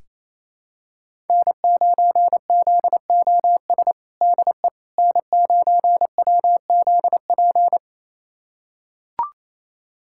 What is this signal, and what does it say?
Morse code, heard as a keyed tone, about 28 wpm, 700 Hz. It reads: N9ZOS DE N9WZP
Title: USA Callsign DE Callsign - 5 Characters - 28wpm